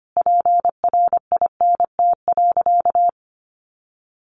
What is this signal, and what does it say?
Morse code, heard as a keyed tone, about 25 wpm, 700 Hz. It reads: PRINT.
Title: Sentences from Top 1400 Words 25wpm